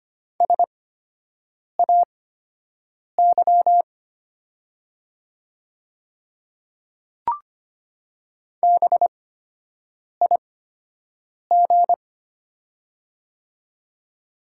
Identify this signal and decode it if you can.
Morse code, heard as a keyed tone, about 25 wpm, spaced as for 7 wpm, 700 Hz. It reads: SAY BIG
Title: 8x Character Spacing - Top 200 Words - 25wpm